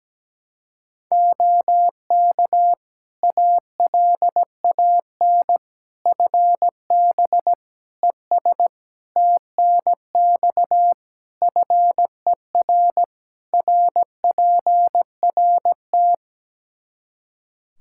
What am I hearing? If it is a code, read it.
Morse code, heard as a keyed tone, about 17 wpm, 700 Hz. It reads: OK ALAN FB ES TNX FER RPRT
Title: Practice QSO - Exchange 2 - Info - 17wpm